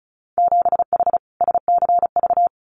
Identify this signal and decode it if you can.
Morse code, heard as a keyed tone, about 34 wpm, 700 Hz. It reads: 7H SCV
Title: Winter Field Day Exchanges Encore 35wpm